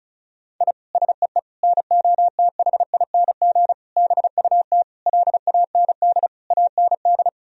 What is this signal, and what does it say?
Morse code, heard as a keyed tone, about 36 wpm, 700 Hz. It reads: I SEE NOTHING BUT LAND AND
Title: Sentences from Top 1000 Words - Review Part 1 - 35wpm